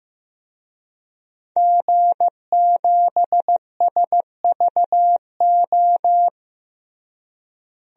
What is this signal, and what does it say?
Morse code, heard as a keyed tone, about 16 wpm, 700 Hz. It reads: G7SVO
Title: DXCC-100 Call Signs - 5 Characters - Repeated 3x - 15wpm